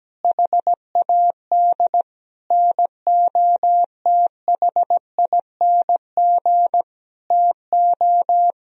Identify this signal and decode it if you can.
Morse code, heard as a keyed tone, about 17 wpm, 700 Hz. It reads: HAD NOTHING TO
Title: Sentences from Top 1000 Words - Review Part 1 - 17wpm